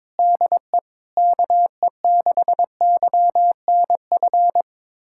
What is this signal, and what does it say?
Morse code, heard as a keyed tone, about 21 wpm, 700 Hz. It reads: DE KE6YNF
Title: USA Callsign DE Callsign 22wpm